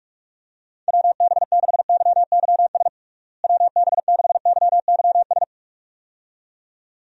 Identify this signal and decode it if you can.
Morse code, heard as a keyed tone, about 46 wpm, 700 Hz. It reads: WB6YYS WB6YYS
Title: USA Call Signs - 6 Characters - Repeated 2x - 45wpm